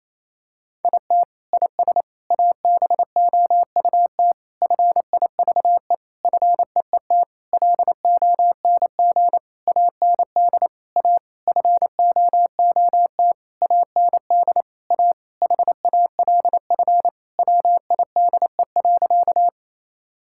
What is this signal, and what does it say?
Morse code, heard as a keyed tone, about 28 wpm, 700 Hz. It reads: IT IS ABOUT FIVE FEET LONG AND A FOOT AND A HALF WIDE.
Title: Sentences from Top 1000 Words - Review Part 6 - 28wpm